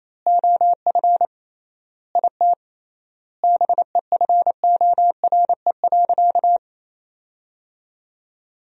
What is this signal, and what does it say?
Morse code, heard as a keyed tone, about 28 wpm, 700 Hz. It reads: OF IT BEFORE.
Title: 3x Word-Spacing - Sentences from Top 400 Words - 28wpm